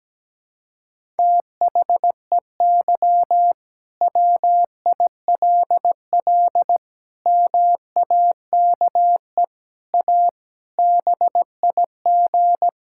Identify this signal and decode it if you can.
Morse code, heard as a keyed tone, about 17 wpm, 700 Hz. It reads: THEY WILL MAKE A BIG
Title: Sentences from Top 1000 Words - Review Part 5 - 17wpm